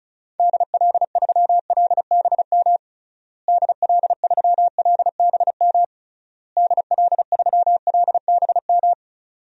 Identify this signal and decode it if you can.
Morse code, heard as a keyed tone, about 36 wpm, 700 Hz. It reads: DL3LBM DL3LBM DL3LBM
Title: DXCC-100 Call Signs - 6 Characters - Repeated 3x - 35wpm